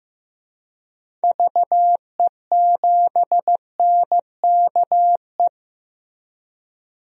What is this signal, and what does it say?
Morse code, heard as a keyed tone, about 15 wpm, 700 Hz. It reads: VE7NKE
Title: DXCC-100 Call Signs - 6 Characters - Repeated 3x - 15wpm